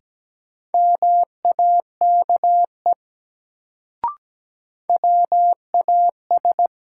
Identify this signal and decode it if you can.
Morse code, heard as a keyed tone, about 17 wpm, 700 Hz. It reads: MAKE WAS
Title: Sentences from Top 100 Words - New Word Drill - 17wpm